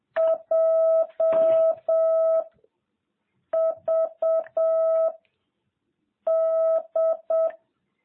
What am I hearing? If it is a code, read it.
Morse code, heard as a keyed tone, about 7 wpm, 660 Hz. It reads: J V D